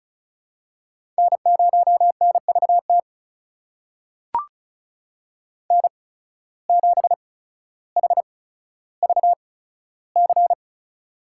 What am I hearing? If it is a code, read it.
Morse code, heard as a keyed tone, about 36 wpm, spaced as for 15 wpm, 700 Hz. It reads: N0NVT N7HVC
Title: USA Call Signs - 5 Characters - 8x Character Spacing - 35wpm